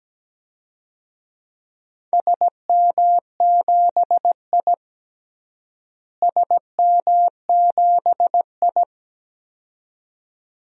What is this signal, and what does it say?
Morse code, heard as a keyed tone, about 17 wpm, 700 Hz. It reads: SM7I SM7I
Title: DXCC-100 Call Signs - 4 Characters - Repeated 2x - 17wpm